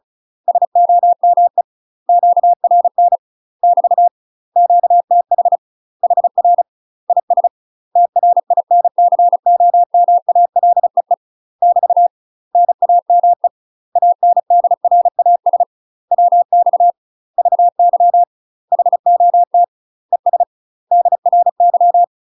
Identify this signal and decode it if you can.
Morse code, heard as a keyed tone, about 36 wpm, 700 Hz. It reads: SOME QRN <BT> QTH HR IS TRINCOMALEE <BT> NAME ANDRAS WX VY HOT ES DRY
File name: UK MASTER QSOs TRAINING / 35WPM / UK MASTER QSOs 0007